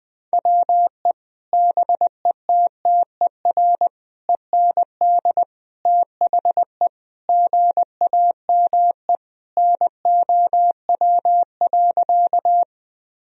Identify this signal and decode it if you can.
Morse code, heard as a keyed tone, about 20 wpm, 700 Hz. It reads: WE BETTER END THE GAME NOW.
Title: Sentences from Top 1000 Words - Review Part 3 - 20wpm